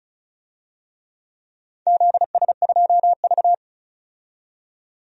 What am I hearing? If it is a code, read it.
Morse code, heard as a keyed tone, about 34 wpm, 700 Hz. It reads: ZS2V
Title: DXCC-100 Call Signs - 4 Characters - 35wpm